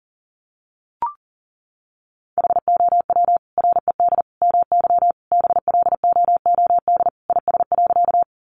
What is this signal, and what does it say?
Morse code, heard as a keyed tone, about 37 wpm, 700 Hz. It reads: HOW RED MY BLOOD IS.
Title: Sentences from Top 1100 Words 40wpm